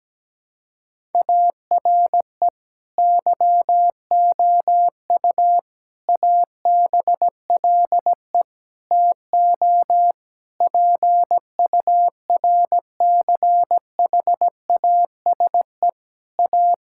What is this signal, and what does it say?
Morse code, heard as a keyed tone, about 17 wpm, 700 Hz. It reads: ARE YOU ABLE TO PURCHASE A
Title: Sentences from Top 1000 Words - Review Part 5 - 17wpm